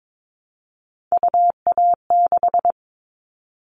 Morse code, heard as a keyed tone, about 22 wpm, 700 Hz. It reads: UA6
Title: Callsign Prefixes for Top 25 DXCC - 22wpm